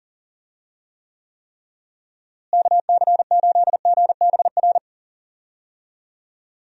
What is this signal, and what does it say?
Morse code, heard as a keyed tone, about 40 wpm, 700 Hz. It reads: KC8GBR
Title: USA Call Signs - 6 Characters - Repeated 3x - 40wpm